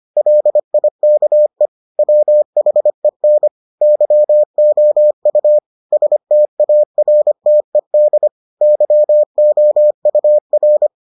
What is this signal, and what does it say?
Morse code, heard as a keyed tone, about 25 wpm, 600 Hz. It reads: LIKE WHEN YOU STARTED YOUR